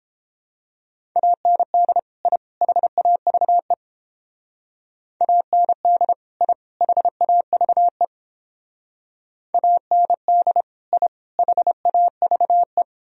Morse code, heard as a keyed tone, about 31 wpm, 700 Hz. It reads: AND I HAVE AND I HAVE AND I HAVE